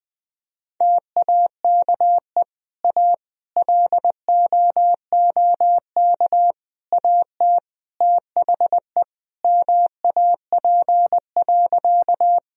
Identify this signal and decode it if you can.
Morse code, heard as a keyed tone, about 20 wpm, 700 Hz. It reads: TAKE A LOOK AT THE MAP.
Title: Sentences from Top 1000 Words - Review Part 2 - 20wpm